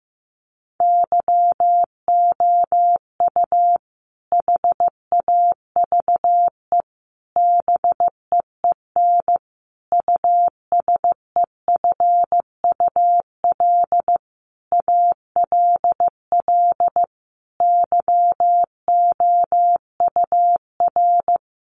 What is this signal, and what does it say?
Morse code, heard as a keyed tone, about 14 wpm, 700 Hz. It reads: YOU HAVE BEEN USEFUL ALL YOUR
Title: Sentences from Top 700 Words 15wpm